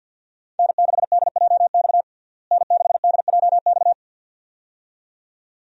Morse code, heard as a keyed tone, about 48 wpm, 700 Hz. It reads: N6DJX N6DJX
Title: USA Call Signs - 5 Characters - Repeated 2x - 50wpm